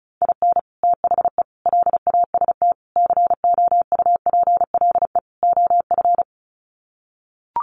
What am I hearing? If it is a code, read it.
Morse code, heard as a keyed tone, about 34 wpm, 700 Hz. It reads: IN THE LAST COUPLE OF
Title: Sets of 5 Words 35wpm